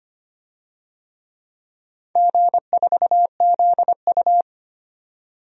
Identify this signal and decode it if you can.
Morse code, heard as a keyed tone, about 25 wpm, 700 Hz. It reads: G4ZU
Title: DXCC-100 Call Signs - 4 Characters - 25wpm